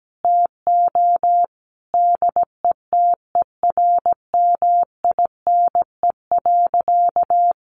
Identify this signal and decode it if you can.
Morse code, heard as a keyed tone, about 17 wpm, 700 Hz. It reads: TO DETERMINE.
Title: Sentences from Top 1500 Words 17wpm